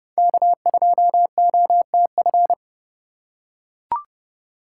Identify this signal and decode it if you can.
Morse code, heard as a keyed tone, about 29 wpm, 700 Hz. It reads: K2OTF